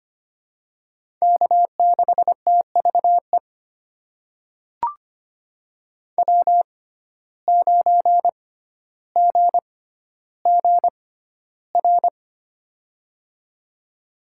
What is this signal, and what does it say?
Morse code, heard as a keyed tone, about 25 wpm, spaced as for 13 wpm, 700 Hz. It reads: K6TVE W9GGR
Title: USA Call Signs - 5 Characters - 6x Character Spacing - 25wpm